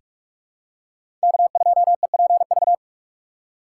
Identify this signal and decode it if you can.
Morse code, heard as a keyed tone, about 46 wpm, 700 Hz. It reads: K2EPV
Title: USA Call Signs - 5 Characters - 3x Character Spacing - 45wpm